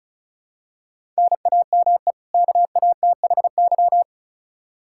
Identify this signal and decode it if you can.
Morse code, heard as a keyed tone, about 34 wpm, 700 Hz. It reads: NAME KATHY
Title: Practice QSO - Exchange 1 - Beginning - 35wpm